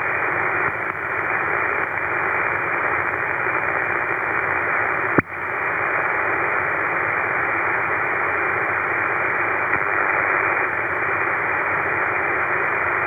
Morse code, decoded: none